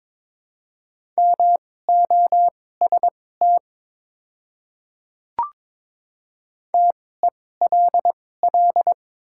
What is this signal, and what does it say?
Morse code, heard as a keyed tone, about 22 wpm, spaced as for 16 wpm, 700 Hz. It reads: MOST TELL
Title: 2x Character Spacing - Top 300 Words - 22wpm